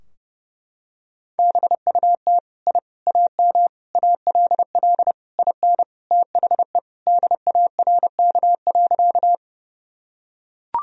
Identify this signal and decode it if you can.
Morse code, heard as a keyed tone, about 30 wpm, 700 Hz. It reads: BUT I AM ALL IN THE DARK.
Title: Sentences from Top 1000 Words - Review Part 2 - 30wpm